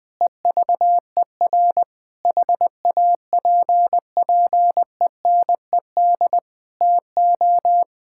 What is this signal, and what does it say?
Morse code, heard as a keyed tone, about 20 wpm, 700 Hz. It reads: EVER HAPPENED TO
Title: Sentences with 8-Word Phrases - 20wpm